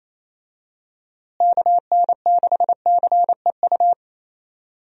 Morse code, heard as a keyed tone, about 28 wpm, 700 Hz. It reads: KN6CEU
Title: USA Call Signs - 6 Characters - Repeated 2x - 28wpm